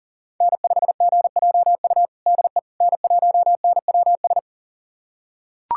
Morse code, heard as a keyed tone, about 40 wpm, 700 Hz. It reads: N5GJU DE N1NWS
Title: USA Callsign DE Callsign - 5 Characters - 40wpm